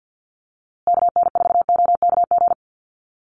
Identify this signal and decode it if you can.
Morse code, heard as a keyed tone, about 50 wpm, 700 Hz. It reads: KN4OKG
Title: USA Call Signs Repeated 3x 50wpm